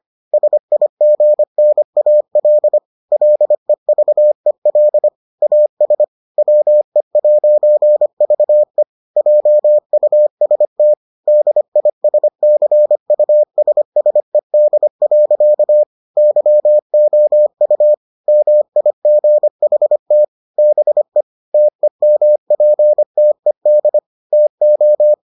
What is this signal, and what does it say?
Morse code, heard as a keyed tone, about 25 wpm, 600 Hz. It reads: SIGNAL LEVEL AS WE'VE JUST DISCUSSED. YOU MIGHT BE TEMPTED TO